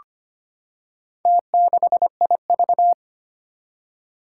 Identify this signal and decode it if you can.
Morse code, heard as a keyed tone, about 25 wpm, 700 Hz. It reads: T6IV
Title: Four Challenging Letter-Number Combinations 25wpm